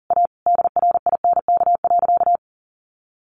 Morse code, heard as a keyed tone, about 40 wpm, 700 Hz. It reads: A DRINK.
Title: Sentences from Top 1500 Words 40wpm